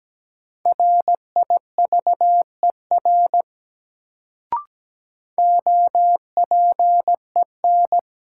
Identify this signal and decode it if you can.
Morse code, heard as a keyed tone, about 17 wpm, 700 Hz. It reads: RIVER OPEN
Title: Sentences from Top 300 Words - New Word Drill - 17wpm